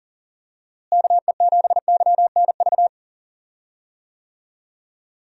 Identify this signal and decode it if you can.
Morse code, heard as a keyed tone, about 37 wpm, 700 Hz. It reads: KE7YNV
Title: USA Call Signs - 6 Characters - Repeated 3x - 40wpm